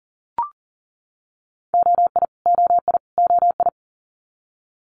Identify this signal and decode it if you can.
Morse code, heard as a keyed tone, about 42 wpm, 690 Hz. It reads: OI OI OI
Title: Multi-letter Phonemes Repeated 3x 40wpm